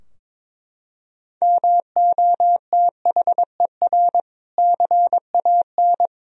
Morse code, heard as a keyed tone, about 22 wpm, 700 Hz. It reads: MOTHER CAN
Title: Sentences from Top 1000 Words - Review Part 4 - 22wpm